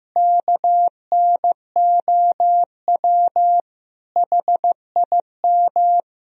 Morse code, heard as a keyed tone, about 15 wpm, 700 Hz. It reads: KNOW HIM
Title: Sentences from Top 1000 Words - Review Part 1 - 15wpm